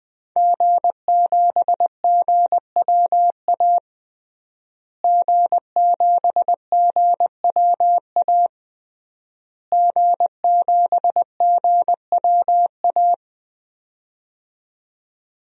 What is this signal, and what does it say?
Morse code, heard as a keyed tone, about 20 wpm, 700 Hz. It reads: G7GWA G7GWA G7GWA
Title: DXCC-100 Call Signs - 5 Characters - Repeated 3x - 20wpm